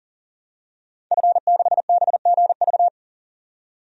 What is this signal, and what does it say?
Morse code, heard as a keyed tone, about 40 wpm, 700 Hz. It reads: F6BGV
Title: DXCC-100 Call Signs - 5 Characters - 40wpm